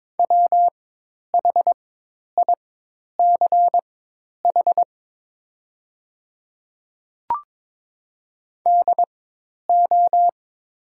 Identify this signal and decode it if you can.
Morse code, heard as a keyed tone, about 22 wpm, spaced as for 10 wpm, 700 Hz. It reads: WHICH DO
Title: 4x Character Spacing - Top 100 Words - 22wpm